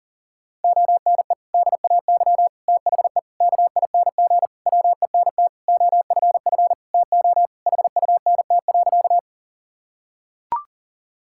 Morse code, heard as a keyed tone, about 40 wpm, 700 Hz. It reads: ONE DAY THE KING WENT OFF TO HUNT.
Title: Sentences from Top 1600 Words 40wpm